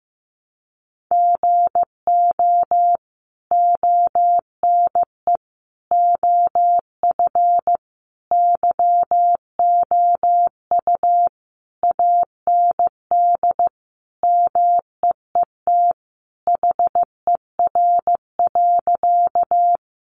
Morse code, heard as a keyed tone, about 15 wpm, 700 Hz. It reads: GO ONE OF YOU AND MEET HER.
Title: Sentences from Top 1100 Words 15wpm